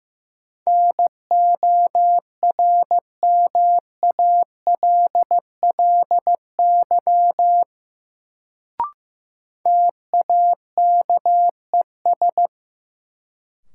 Morse code, heard as a keyed tone, about 15 wpm, 700 Hz. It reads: NORMALLY TAKES
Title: Sentences from Top 1000 Words - Review Part 6 - New Word Drill - 15wpm